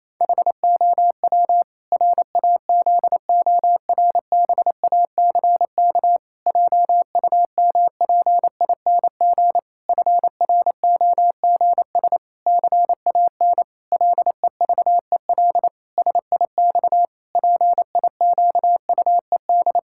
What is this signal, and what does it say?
Morse code, heard as a keyed tone, about 28 wpm, 700 Hz. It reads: HOW RAZORBACK JUMPING FROGS CAN LEVEL SIX PIQUED
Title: Pangram Challenge v2 - 28wpm